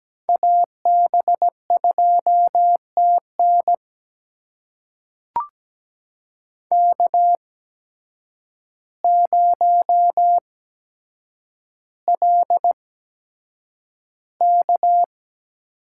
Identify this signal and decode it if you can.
Morse code, heard as a keyed tone, about 17 wpm, spaced as for 8 wpm, 700 Hz. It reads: AB2TN K0LK
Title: USA Call Signs - 8x Character Spacing - 17wpm